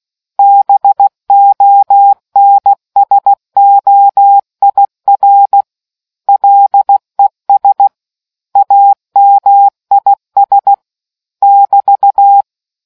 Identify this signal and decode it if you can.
Morse code, heard as a keyed tone, about 16 wpm, 800 Hz. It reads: BONSOIR LES AMIS <BT>